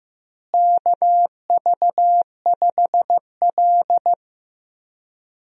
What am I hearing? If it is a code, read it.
Morse code, heard as a keyed tone, about 15 wpm, 700 Hz. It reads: KV5L
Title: USA Call Signs - 4 Characters - Repeated 3x - 15wpm